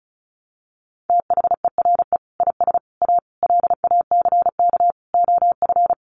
Morse code, heard as a keyed tone, about 34 wpm, 700 Hz. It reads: THERE IS A LACK OF
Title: Sentences from Top 1100 Words 35wpm